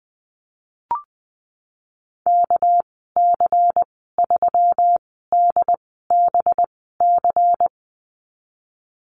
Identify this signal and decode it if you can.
Morse code, heard as a keyed tone, about 19 wpm, spaced as for 14 wpm, 700 Hz. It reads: KC3DBC
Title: 200 Lesson 32g - Letter k - Call Signs 20wpm